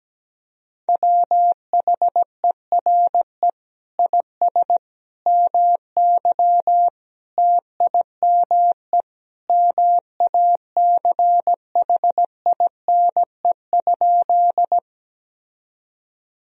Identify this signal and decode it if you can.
Morse code, heard as a keyed tone, about 17 wpm, 700 Hz. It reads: WHERE IS MY TIME MACHINE?
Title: Sentences from Top 1000 Words - Review Part 2 - 17wpm